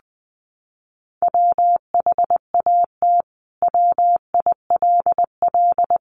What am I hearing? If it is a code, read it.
Morse code, heard as a keyed tone, about 19 wpm, 700 Hz. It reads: WHAT WILL
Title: Sentences from Top 1100 Words 20wpm